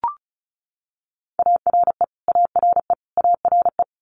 Morse code, heard as a keyed tone, about 36 wpm, 700 Hz. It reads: ARE ARE ARE
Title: Multi-letter Phonemes Repeated 3x 35wpm